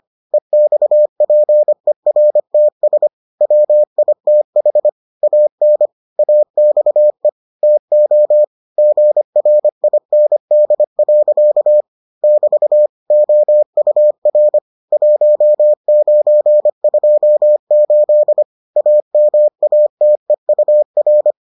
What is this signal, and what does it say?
Morse code, heard as a keyed tone, about 25 wpm, 600 Hz. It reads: EXPERTS WITH AN AXE TO GRIND. <BT> OUR 1928 AMATEUR